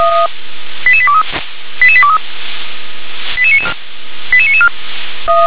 Five-tone selective call (ZVEI), heard as quick runs of five tones, tones of about 70 ms. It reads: 88823 88823 88994